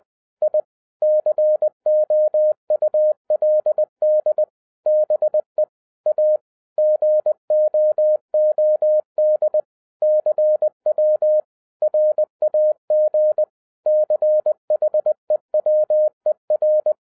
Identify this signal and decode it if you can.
Morse code, heard as a keyed tone, about 20 wpm, 610 Hz. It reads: I COULD BE A GOOD CW RAG CHEWER
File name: 2012-01-mp3 / n05-CAME BACK TO HAM RADIO AFTER 30 YEARS OF INTERVAL-20wpm